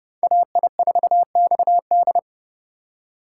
Morse code, heard as a keyed tone, about 40 wpm, 700 Hz. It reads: AI4XD